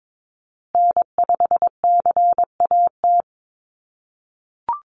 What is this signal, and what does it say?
Morse code, heard as a keyed tone, about 22 wpm, 700 Hz. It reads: N5CAT
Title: USA Call Signs Repeated 2x 22wpm